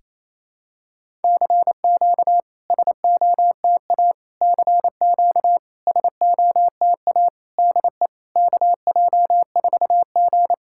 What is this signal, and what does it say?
Morse code, heard as a keyed tone, about 28 wpm, 700 Hz. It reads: CQ SOTA CQ SOTA DE KJ4G
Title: SOTA QSO Practice 28wpm